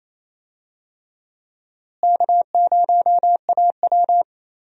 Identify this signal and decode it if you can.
Morse code, heard as a keyed tone, about 28 wpm, 700 Hz. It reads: K0AW
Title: USA Call Signs - 4 Characters - Repeated 3x - 28wpm